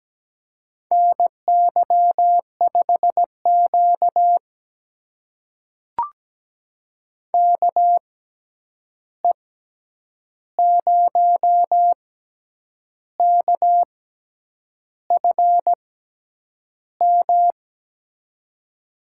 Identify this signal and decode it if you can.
Morse code, heard as a keyed tone, about 17 wpm, spaced as for 8 wpm, 700 Hz. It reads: NY5Q KE0KFM
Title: USA Call Signs - 6x Character Spacing - 17wpm